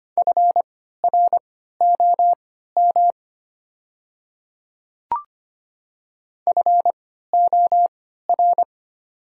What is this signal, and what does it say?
Morse code, heard as a keyed tone, about 25 wpm, spaced as for 14 wpm, 700 Hz. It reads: FROM FOR
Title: 3x Character Spacing - Top 200 Words - 25wpm